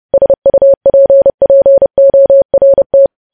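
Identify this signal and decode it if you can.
Morse code, heard as a keyed tone, about 30 wpm, 550 Hz. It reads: SUPPORT
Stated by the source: News headlines sent at 30 wpm.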